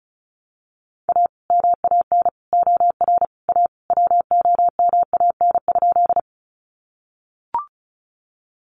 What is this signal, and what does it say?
Morse code, heard as a keyed tone, about 34 wpm, 700 Hz. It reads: A MAN OR A WOMAN?
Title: Sentences from Top 1100 Words 35wpm